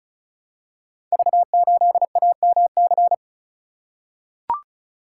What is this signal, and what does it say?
Morse code, heard as a keyed tone, about 36 wpm, 690 Hz. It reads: V8AMC
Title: Code Groups - 5 Letter:Numbers - 35wpm